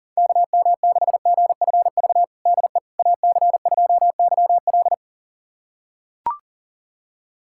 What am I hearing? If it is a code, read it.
Morse code, heard as a keyed tone, about 40 wpm, 700 Hz. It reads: KM6GFV DE AC2YL
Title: USA Callsign DE Callsign 40wpm